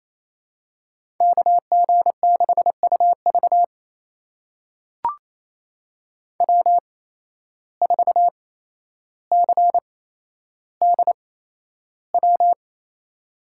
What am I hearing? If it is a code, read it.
Morse code, heard as a keyed tone, about 28 wpm, spaced as for 12 wpm, 700 Hz. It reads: KG6UV W4CDW
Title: USA Call Signs - 5 Characters - 8x Character Spacing - 28wpm